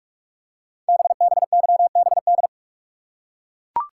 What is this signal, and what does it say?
Morse code, heard as a keyed tone, about 44 wpm, 700 Hz. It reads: BBYBD
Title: Code Groups - 5 Letters - 45wpm